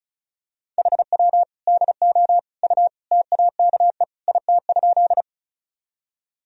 Morse code, heard as a keyed tone, about 35 wpm, 700 Hz. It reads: HW DO U TAKE IT?
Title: Sentences from Top 200 Words and Abbreviations 35wpm